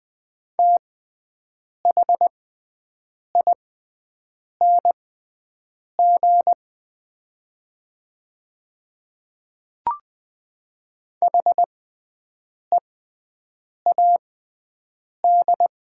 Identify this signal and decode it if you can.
Morse code, heard as a keyed tone, about 20 wpm, spaced as for 7 wpm, 700 Hz. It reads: THING HEAD
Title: 6x Character Spacing - Top 300 Words - 20wpm